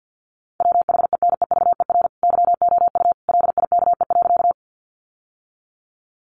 Morse code, heard as a keyed tone, about 50 wpm, 700 Hz. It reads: WHENEVER YOU LIKE.
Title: Sentences from Top 1200 Words 50wpm